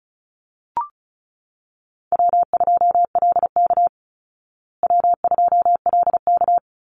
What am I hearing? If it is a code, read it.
Morse code, heard as a keyed tone, about 34 wpm, 700 Hz. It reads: W2LK W2LK
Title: Call Signs Repeated 3x - 35wpm